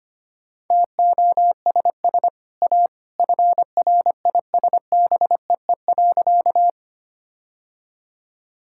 Morse code, heard as a keyed tone, about 25 wpm, 700 Hz. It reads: TOSS A FRISBEE.
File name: Easy Sentences with Hints - 25wpm